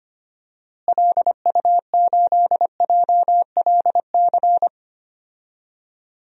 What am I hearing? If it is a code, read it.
Morse code, heard as a keyed tone, about 25 wpm, 700 Hz. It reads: LU8JLC
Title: DXCC-100 Call Signs - 6 Characters - Repeated 2x - 25wpm